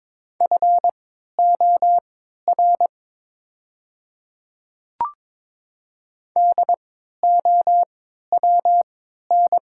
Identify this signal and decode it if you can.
Morse code, heard as a keyed tone, about 22 wpm, spaced as for 12 wpm, 700 Hz. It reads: FOR DOWN
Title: 3x Character Spacing - Top 100 Words - 22wpm